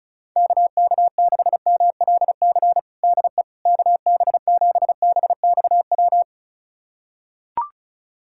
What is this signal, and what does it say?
Morse code, heard as a keyed tone, about 36 wpm, 690 Hz. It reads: KK6MLC DE KB7BXW
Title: USA Callsign DE Callsign 35wpm